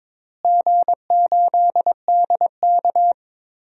Morse code, heard as a keyed tone, about 21 wpm, 700 Hz. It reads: G8DK
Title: DXCC-100 Call Signs - 4 Characters - Repeated 3x - 22wpm